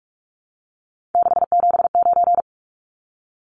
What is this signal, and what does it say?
Morse code, heard as a keyed tone, about 44 wpm, 700 Hz. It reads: 679
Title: Numbers - 3-Digit 45wpm